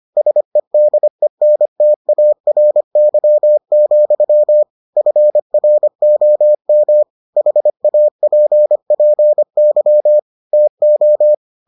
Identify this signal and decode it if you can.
Morse code, heard as a keyed tone, about 25 wpm, 600 Hz. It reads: SEDENTARY, FROM HAPPY TO